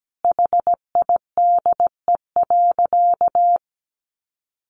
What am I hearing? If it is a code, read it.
Morse code, heard as a keyed tone, about 17 wpm, 700 Hz. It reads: HIDE.
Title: Easy Phrases 17wpm